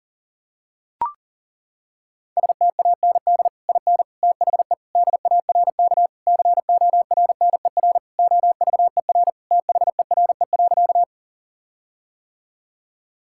Morse code, heard as a keyed tone, about 40 wpm, 700 Hz. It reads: STAND IN THE DARK CORNER OVER THERE.
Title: Sentences from Top 1600 Words 40wpm